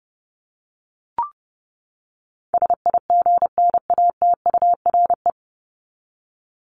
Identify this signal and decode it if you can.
Morse code, heard as a keyed tone, about 30 wpm, 700 Hz. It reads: SIGNATURE
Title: Top 1751-2000 Words 30wpm